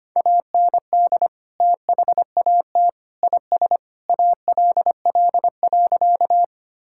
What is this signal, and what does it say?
Morse code, heard as a keyed tone, about 25 wpm, 700 Hz. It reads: AND THAT IS ALL.
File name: practice / 25wpm / Sentences with 3-Word Phrases - 25wpm